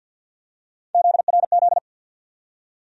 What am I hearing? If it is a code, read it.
Morse code, heard as a keyed tone, about 51 wpm, 700 Hz. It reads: ZRZ
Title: USA Call Sign Postfixes - 3 Letters 50wpm